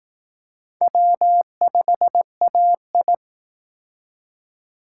Morse code, heard as a keyed tone, about 18 wpm, 700 Hz. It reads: W5AI
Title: Speed Racing - USA 1x2 Call Signs 15wpm